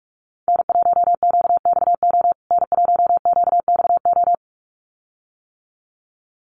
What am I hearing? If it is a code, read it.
Morse code, heard as a keyed tone, about 44 wpm, 700 Hz. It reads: N1QXO N1QXO
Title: USA Call Signs Repeated 2x 45wpm